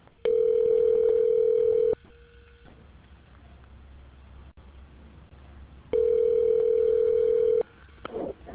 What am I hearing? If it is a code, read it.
Ringback tone (North America), 440 + 480 Hz.